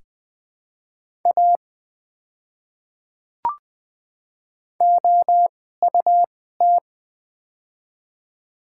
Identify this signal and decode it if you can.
Morse code, heard as a keyed tone, about 20 wpm, spaced as for 15 wpm, 700 Hz. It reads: A OUT
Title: 2x Character Spacing - Top 200 Words - 20wpm